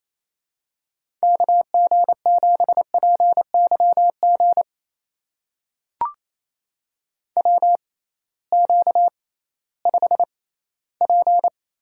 Morse code, heard as a keyed tone, about 28 wpm, spaced as for 16 wpm, 700 Hz. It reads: KG7PYG WQ5P
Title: USA Call Signs - 6x Character Spacing - 28wpm